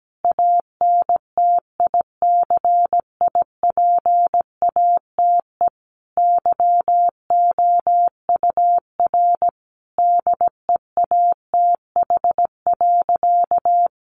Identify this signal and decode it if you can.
Morse code, heard as a keyed tone, about 17 wpm, 700 Hz. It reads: ANTICIPATE YOUR DEATH.
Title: Sentences from Top 1500 Words 17wpm